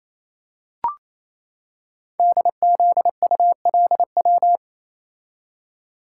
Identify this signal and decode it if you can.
Morse code, heard as a keyed tone, about 29 wpm, 700 Hz. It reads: DZULW
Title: Code Groups - 5 Letters - 28wpm